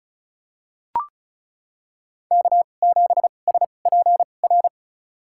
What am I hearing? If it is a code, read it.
Morse code, heard as a keyed tone, about 34 wpm, spaced as for 25 wpm, 700 Hz. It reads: K7SPR
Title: USA Call Signs - 5 Characters - 2x Character Spacing - 35wpm